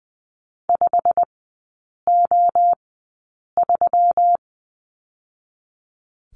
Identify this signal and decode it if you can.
Morse code, heard as a keyed tone, about 20 wpm, spaced as for 8 wpm, 690 Hz. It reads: 5O3
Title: 159 Lesson 26e - Letter g - Three Characters 20wpm